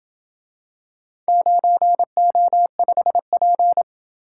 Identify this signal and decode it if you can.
Morse code, heard as a keyed tone, about 28 wpm, 700 Hz. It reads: 9O5P